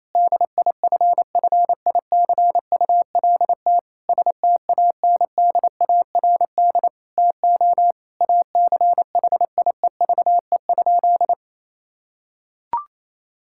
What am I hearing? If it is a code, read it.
Morse code, heard as a keyed tone, about 28 wpm, 700 Hz. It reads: DIFFICULT STANDARD TO ACHIEVE?
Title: Sentences from Top 1000 Words - Review Part 6 - 28wpm